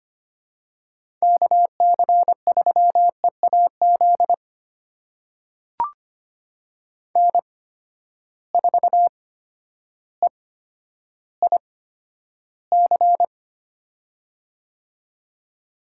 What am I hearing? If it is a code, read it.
Morse code, heard as a keyed tone, about 25 wpm, spaced as for 11 wpm, 700 Hz. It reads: KC3EAZ N4EIC
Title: USA Call Signs - 8x Character Spacing - 25wpm